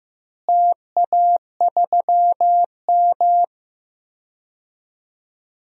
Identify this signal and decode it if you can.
Morse code, heard as a keyed tone, about 15 wpm, 700 Hz. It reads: TA3M
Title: DXCC-100 Call Signs - 4 Characters - Repeated 3x - 15wpm